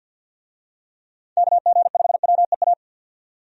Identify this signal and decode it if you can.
Morse code, heard as a keyed tone, about 50 wpm, 700 Hz. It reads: KG5WEA
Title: USA Call Signs - 6 Characters - 50wpm